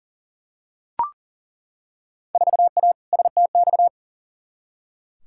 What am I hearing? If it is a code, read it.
Morse code, heard as a keyed tone, about 38 wpm, 700 Hz. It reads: 4A STX
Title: Field Day Exchanges - 40wpm